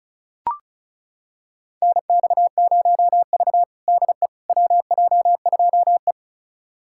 Morse code, heard as a keyed tone, about 36 wpm, 700 Hz. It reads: NX0V DE WJ2E
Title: USA Callsign DE Callsign - 4 Characters - 35wpm